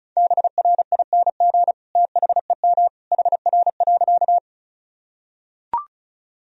Morse code, heard as a keyed tone, about 36 wpm, 700 Hz. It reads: BRING THEM HR.
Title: Sentences from Top 500 Words and Abbreviations 35wpm